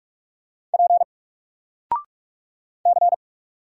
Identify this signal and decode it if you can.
Morse code, heard as a keyed tone, about 44 wpm, 700 Hz. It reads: P C
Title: Single Letter-Number Alt 45wpm